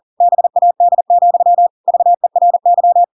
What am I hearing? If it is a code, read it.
Morse code, heard as a keyed tone, about 40 wpm, 700 Hz. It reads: BAD, VERY
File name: Pinocchio / Pinocchio ch7 40 40 ik1hge